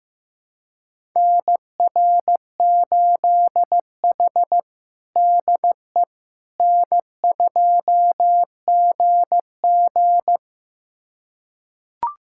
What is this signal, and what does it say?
Morse code, heard as a keyed tone, about 16 wpm, 700 Hz. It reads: NR8H DE N2GG
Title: USA Callsign DE Callsign - 4 Characters - 15wpm